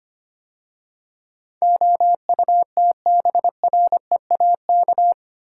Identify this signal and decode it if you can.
Morse code, heard as a keyed tone, about 25 wpm, 700 Hz. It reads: OUTBREAK
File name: 8 Letter Words - First 3 Letters Given - 25wpm